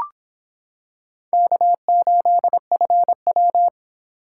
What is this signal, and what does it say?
Morse code, heard as a keyed tone, about 25 wpm, 700 Hz. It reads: K8FW